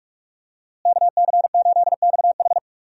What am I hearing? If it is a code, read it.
Morse code, heard as a keyed tone, about 41 wpm, 700 Hz. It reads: KC8XH
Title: USA Call Signs - 5 Characters - Repeated 3x - 45wpm